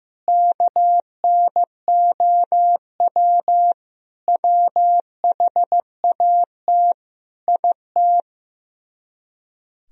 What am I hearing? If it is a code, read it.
Morse code, heard as a keyed tone, about 16 wpm, 700 Hz. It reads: KNOW WHAT IT
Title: Speed-Racing - Sets of 5 Words 15wpm